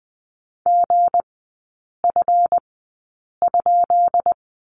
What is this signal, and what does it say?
Morse code, heard as a keyed tone, about 20 wpm, spaced as for 8 wpm, 700 Hz. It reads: GF?
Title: 212 Lesson 34e - Number 8 - Three Characters 20wpm